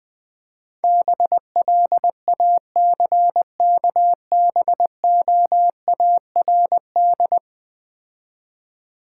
Morse code, heard as a keyed tone, about 21 wpm, 700 Hz. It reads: BLACKBOARD
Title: Easy Words - Speed-Racing with Hints - 20wpm